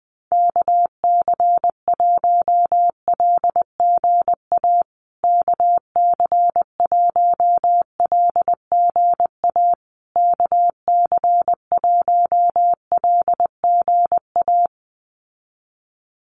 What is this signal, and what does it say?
Morse code, heard as a keyed tone, about 20 wpm, 700 Hz. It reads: KC1LGA KC1LGA KC1LGA
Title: USA Call Signs Repeated 3x 20wpm